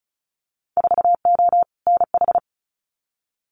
Morse code, heard as a keyed tone, about 36 wpm, 700 Hz. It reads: 4O NH
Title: Winter Field Day Exchanges Encore 35wpm